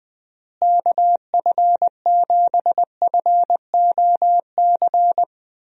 Morse code, heard as a keyed tone, about 20 wpm, 700 Hz. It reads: KF7FOC
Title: USA Call Signs - 6 Characters - Repeated 2x - 20wpm